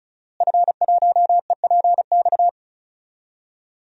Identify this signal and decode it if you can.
Morse code, heard as a keyed tone, about 36 wpm, 700 Hz. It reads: F1EPX
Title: DXCC-100 Call Signs - 5 Characters - Repeated 2x - 35wpm